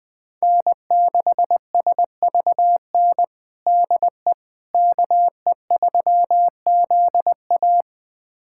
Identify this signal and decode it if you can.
Morse code, heard as a keyed tone, about 20 wpm, 700 Hz. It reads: N6SVN DE KE3ZA
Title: USA Callsign DE Callsign - 5 Characters - 20wpm